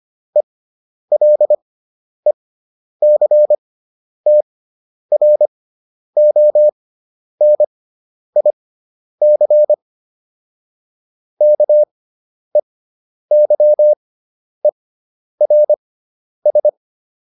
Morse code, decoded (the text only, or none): ELECTRONIC KEYERS